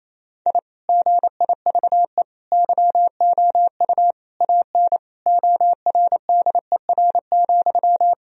Morse code, heard as a keyed tone, about 28 wpm, 700 Hz. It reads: I GIVE YOU AN ORDER,
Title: Sentences from Top 1000 Words - Review Part 2 - 28wpm